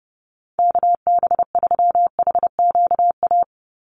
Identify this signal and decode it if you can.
Morse code, heard as a keyed tone, about 30 wpm, 700 Hz. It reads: KB3HQA
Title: USA Call Signs 30wpm